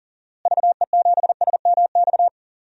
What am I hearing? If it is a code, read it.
Morse code, heard as a keyed tone, about 40 wpm, 700 Hz. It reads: VE7SMX
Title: DXCC-100 Call Signs - 6 Characters - Repeated 2x - 40wpm